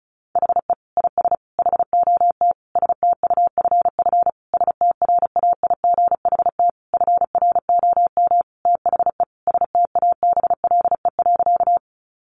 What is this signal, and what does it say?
Morse code, heard as a keyed tone, about 34 wpm, 700 Hz. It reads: HE IS HOT STUFF STRAIGHT FROM THE STABLE.
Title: Sentences from Top 1500 Words 35wpm